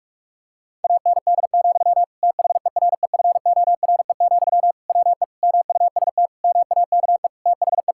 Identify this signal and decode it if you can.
Morse code, heard as a keyed tone, about 46 wpm, 690 Hz. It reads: AND, THEREFORE, WE MUST MAKE THE
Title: Sentences from Top 1000 Words - Review Part 5 - 45wpm